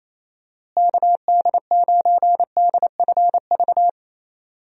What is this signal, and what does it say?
Morse code, heard as a keyed tone, about 28 wpm, 700 Hz. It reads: KD9DFV
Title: USA Call Signs - 6 Characters - Repeated 3x - 28wpm